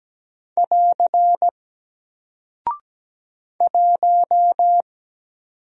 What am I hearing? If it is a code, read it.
Morse code, heard as a keyed tone, about 17 wpm, 700 Hz. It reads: <AR> 1
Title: Single Character (Letter:Number:Punctuation:Prosign) Alt 17wpm